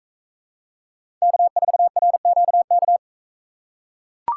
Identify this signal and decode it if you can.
Morse code, heard as a keyed tone, about 42 wpm, 700 Hz. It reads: K4RQK